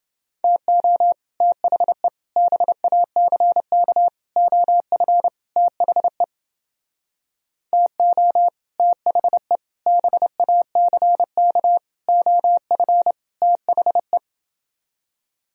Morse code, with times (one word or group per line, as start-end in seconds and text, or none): 0.4-1.1 TO
1.4-2.1 THE
2.4-4.1 BACK
4.4-5.3 OF
5.6-6.2 THE
7.7-8.5 TO
8.8-9.6 THE
9.9-11.8 BACK
12.1-13.1 OF
13.4-14.2 THE